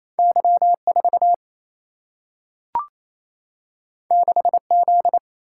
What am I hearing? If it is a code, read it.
Morse code, heard as a keyed tone, about 28 wpm, 700 Hz. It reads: Y4 6Z
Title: Single-Letter Single-Number 28wpm